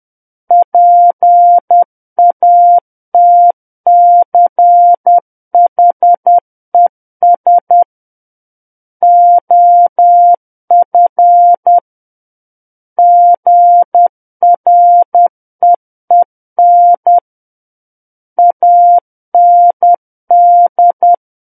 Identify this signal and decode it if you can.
Morse code, heard as a keyed tone, about 10 wpm, 700 Hz. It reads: PATCHES OF GREEN AND
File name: Book One / 01-Chapter 1 - 10 WPM